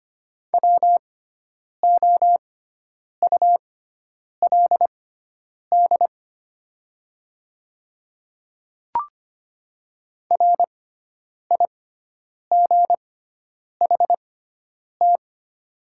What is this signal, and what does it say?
Morse code, heard as a keyed tone, about 25 wpm, spaced as for 9 wpm, 690 Hz. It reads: WOULD RIGHT
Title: 6x Character Spacing - Top 200 Words - 25wpm